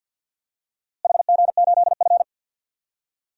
Morse code, heard as a keyed tone, about 46 wpm, 700 Hz. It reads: HG8F
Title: DXCC-100 Call Signs - 4 Characters - Repeated 2x - 50wpm